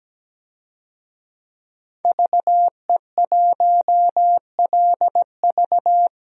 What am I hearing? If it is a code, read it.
Morse code, heard as a keyed tone, about 17 wpm, 700 Hz. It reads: VE1LV
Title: DXCC-100 Call Signs - 5 Characters - 17wpm